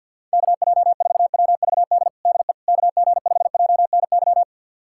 Morse code, heard as a keyed tone, about 47 wpm, 700 Hz. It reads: KJ4WVD DE KG5JNY
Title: USA Callsign DE Callsign - 6 Characters - 50wpm